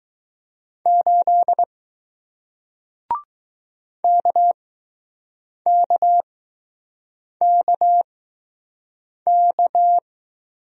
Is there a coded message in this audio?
Yes: Morse code from 0.9 to 10.0 s, heard as a keyed tone, about 20 wpm, spaced as for 6 wpm, 700 Hz. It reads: 8 KKKK